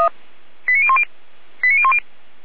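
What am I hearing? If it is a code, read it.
Five-tone selective call (ZVEI), heard as quick runs of five tones, tones of about 70 ms. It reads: 89010 89010
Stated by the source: ZVEI five-tone call received by SDR, fire brigade Feuerwehr Orlamünde.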